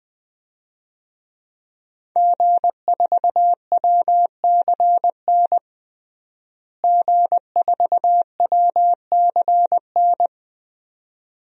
Morse code, heard as a keyed tone, about 21 wpm, 700 Hz. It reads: G4WCN G4WCN
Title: DXCC-100 Call Signs - 5 Characters - Repeated 2x - 20wpm